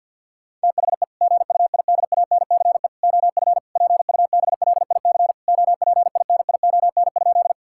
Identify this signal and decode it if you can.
Morse code, heard as a keyed tone, about 50 wpm, 700 Hz. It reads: THE GUIDANCE OF PUBLIC OPINION?
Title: Sentences from Top 1600 Words 50wpm